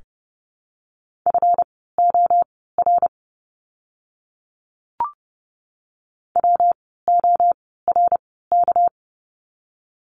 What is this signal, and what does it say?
Morse code, heard as a keyed tone, about 30 wpm, spaced as for 17 wpm, 700 Hz. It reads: FOR WORK
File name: course / 30:10wpm / 237 Lesson 38f - Letter q - Words 30wpm